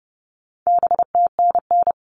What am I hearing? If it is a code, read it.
Morse code, heard as a keyed tone, about 30 wpm, 700 Hz. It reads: BTNN
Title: All Cut Numbers - 4-Digit 30wpm